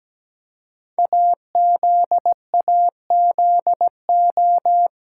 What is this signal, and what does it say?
Morse code, heard as a keyed tone, about 17 wpm, 700 Hz. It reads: AZAZO
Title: Code Groups - 5 Letters - 17wpm